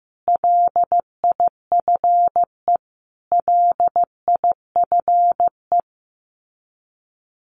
Morse code, heard as a keyed tone, about 15 wpm, 700 Hz. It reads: LIFE LIFE
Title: Top 300 Words Repeated 15wpm